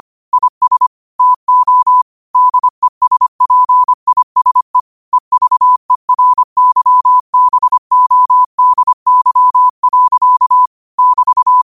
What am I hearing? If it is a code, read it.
Morse code, heard as a keyed tone, about 25 wpm, 1000 Hz. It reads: IS TO DESPISE EVERYBODY. <BT>
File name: qotdfiles / 25wpm